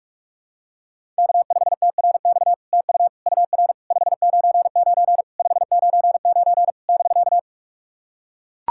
Morse code, heard as a keyed tone, about 44 wpm, 700 Hz. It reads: K5TRX TU UR 599 599 <BK>
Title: SOTA QSO Practice 45wpm